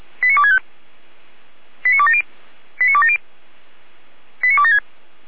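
Five-tone selective call (ZVEI), heard as quick runs of five tones, tones of about 70 ms. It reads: 89265 89270 89270 89276
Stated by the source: ZVEI five-tone selective call received on SDR, from the Feuerwehr Orlamünde audio server.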